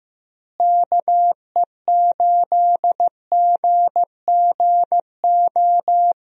Morse code, heard as a keyed tone, about 16 wpm, 700 Hz. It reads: KE8GGO
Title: USA Call Signs - 6 Characters - Repeated 2x - 15wpm